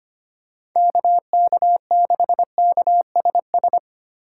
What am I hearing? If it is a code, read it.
Morse code, heard as a keyed tone, about 25 wpm, 700 Hz. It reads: KK6KSS
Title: USA Call Signs - 4x Character Spacing - 25wpm